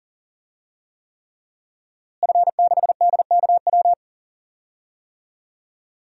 Morse code, heard as a keyed tone, about 40 wpm, 700 Hz. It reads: F6DKW
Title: DXCC-100 Call Signs - 5 Characters - Repeated 3x - 40wpm